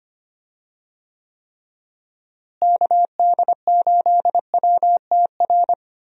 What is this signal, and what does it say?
Morse code, heard as a keyed tone, about 25 wpm, 700 Hz. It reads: KD8WTR
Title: USA Call Signs - 6 Characters - Repeated 3x - 25wpm